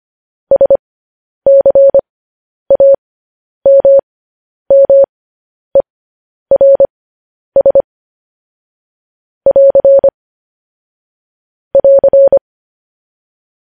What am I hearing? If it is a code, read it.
Morse code, heard as a keyed tone, about 25 wpm, spaced as for 10 wpm, 550 Hz. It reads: SCAMMERS <AR> <AR>
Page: news-headlines-in-morse-code-at-10-wpm